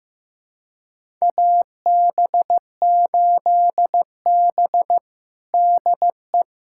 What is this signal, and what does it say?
Morse code, heard as a keyed tone, about 15 wpm, 700 Hz. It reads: AB8B DE
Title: USA Callsign DE Callsign - 4 Characters - 15wpm